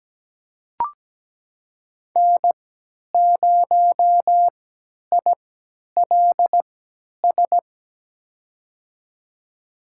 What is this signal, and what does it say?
Morse code, heard as a keyed tone, about 17 wpm, spaced as for 10 wpm, 700 Hz. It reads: N0ILS
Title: USA Call Signs - 5 Characters - 3x Character Spacing - 17wpm